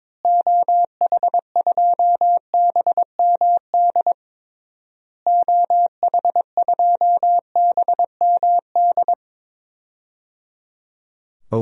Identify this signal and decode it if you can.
Morse code, heard as a keyed tone, about 22 wpm, 700 Hz. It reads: OH2BMD OH2BMD
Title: DXCC-100 Call Signs - 6 Characters - Repeated 3x - 22wpm